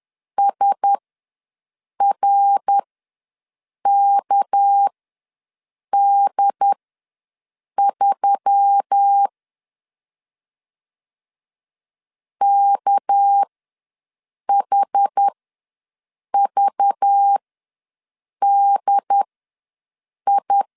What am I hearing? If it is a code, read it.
Morse code, heard as a keyed tone, about 10 wpm, spaced as for 6 wpm, 800 Hz. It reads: SRKD3 KHVDI
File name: LEZIONI / 6 PKX0 / 6 PK ws 10wpm 3745